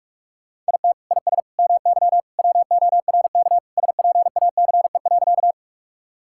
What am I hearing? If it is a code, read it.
Morse code, heard as a keyed tone, about 41 wpm, 700 Hz. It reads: IT IS MY WORK SPACE.
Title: Sentences from Top 1000 Words - Review Part 5 - 45wpm